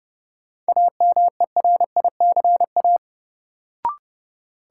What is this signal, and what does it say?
Morse code, heard as a keyed tone, about 30 wpm, 700 Hz. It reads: AMERICA
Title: Sets of 5 Words - New Word Drill - 30wpm